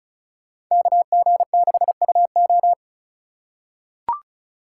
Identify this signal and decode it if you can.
Morse code, heard as a keyed tone, about 36 wpm, 700 Hz. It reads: KG6UO
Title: USA Call Signs - 5 Characters - 4x Character Spacing - 35wpm